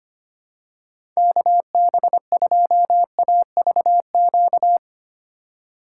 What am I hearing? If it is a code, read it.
Morse code, heard as a keyed tone, about 25 wpm, 700 Hz. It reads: KB2AVQ
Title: USA Call Signs - 6 Characters - 2x Character Spacing - 25wpm